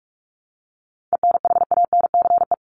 Morse code, heard as a keyed tone, about 44 wpm, 700 Hz. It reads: ENHANCE
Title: Top 1251-1500 Words 45wpm